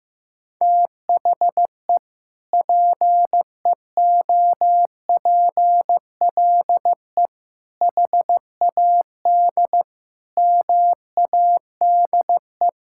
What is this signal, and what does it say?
Morse code, heard as a keyed tone, about 16 wpm, 700 Hz. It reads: THE PEOPLE HAD MADE
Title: Sentences from Top 1000 Words - Review Part 6 - 15wpm